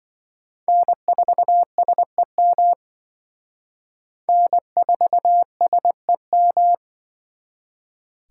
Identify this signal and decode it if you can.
Morse code, heard as a keyed tone, about 25 wpm, 700 Hz. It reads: N4SEM N4SEM